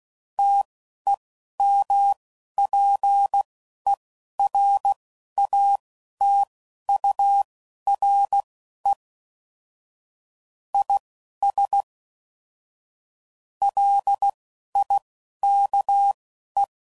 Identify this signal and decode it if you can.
Morse code, heard as a keyed tone, about 16 wpm, spaced as for 11 wpm, 800 Hz. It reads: TEMPERATURE IS LIKE
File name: CW Code Groups @10WPM/Weather Sets